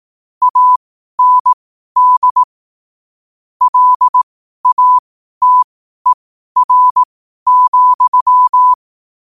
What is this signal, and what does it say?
Morse code, heard as a keyed tone, about 18 wpm, spaced as for 12 wpm, 1000 Hz. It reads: AND LATER,
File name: qotdfiles / 12wpm